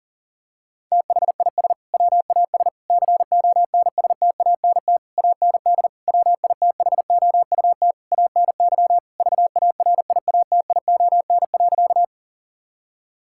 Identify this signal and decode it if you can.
Morse code, heard as a keyed tone, about 40 wpm, 700 Hz. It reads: THIS WAS CONSTANT AND WITHOUT ANY VARIATION.
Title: Sentences from Top 1600 Words 40wpm